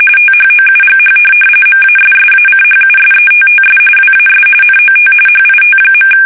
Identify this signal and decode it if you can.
RTTY, about 45.45 baud. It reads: THOUGH MASSACHUSETTS WAS NOT HIT.
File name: fsk850 / USS-Massachusetts